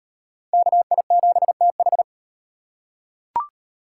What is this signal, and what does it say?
Morse code, heard as a keyed tone, about 33 wpm, 700 Hz. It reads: KI7TH